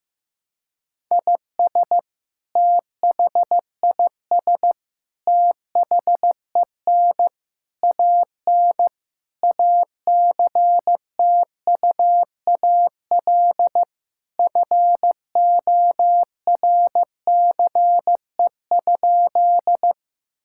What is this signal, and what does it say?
Morse code, heard as a keyed tone, about 15 wpm, 700 Hz. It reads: IS THIS THEN AN ACTUAL FORCE?
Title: Sentences from Top 1000 Words - Review Part 5 - 15wpm